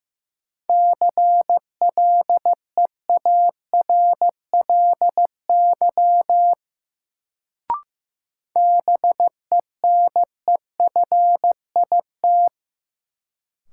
Morse code, heard as a keyed tone, about 16 wpm, 700 Hz. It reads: CLEARLY BENEFIT
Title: Sentences from Top 1300 Words - New Word Drill Alt - 15wpm